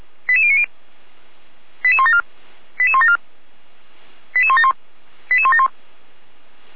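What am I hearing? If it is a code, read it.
Five-tone selective call (ZVEI), heard as quick runs of five tones, tones of about 70 ms. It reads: 88090 88174 88174 88171 88171
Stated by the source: ZVEI five-tone selective call received on an SDR by the Feuerwehr Orlamünde fire brigade.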